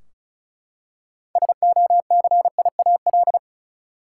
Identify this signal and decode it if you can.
Morse code, heard as a keyed tone, about 34 wpm, 700 Hz. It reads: SOCIAL
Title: Practice QSO - Exchange 3 - More Info - 35wpm